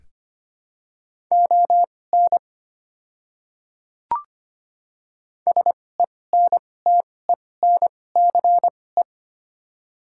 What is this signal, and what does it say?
Morse code, heard as a keyed tone, about 25 wpm, spaced as for 18 wpm, 700 Hz. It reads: ON SENTENCE
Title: 2x Character Spacing - Top 200 Words - 25wpm